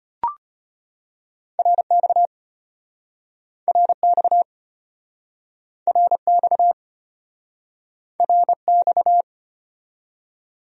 Morse code, heard as a keyed tone, about 33 wpm, 700 Hz. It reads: RX RX RX RX